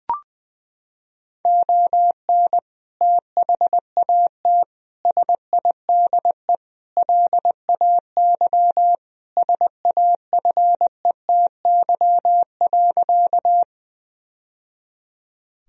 Morse code, heard as a keyed tone, about 20 wpm, 700 Hz. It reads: ON THAT SIDE LAY SAFETY.
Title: Sentences from Top 1000 Words - Review Part 4 - 20wpm